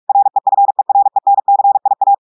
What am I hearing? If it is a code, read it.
Morse code, heard as a keyed tone, about 46 wpm, 800 Hz. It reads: REFERENCIA
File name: portugais texte en clair 1 vitesse 45 mots minutes